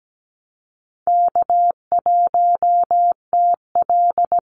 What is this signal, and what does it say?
Morse code, heard as a keyed tone, about 17 wpm, 700 Hz. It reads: K1TL
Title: Call Signs - 17wpm